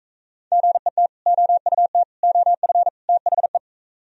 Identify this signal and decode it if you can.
Morse code, heard as a keyed tone, about 41 wpm, 700 Hz. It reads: GET OUT OF THE